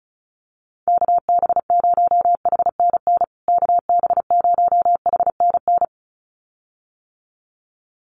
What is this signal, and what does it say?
Morse code, heard as a keyed tone, about 34 wpm, 700 Hz. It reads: KB0HNN KB0HNN
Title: USA Call Signs Repeated 2x 35wpm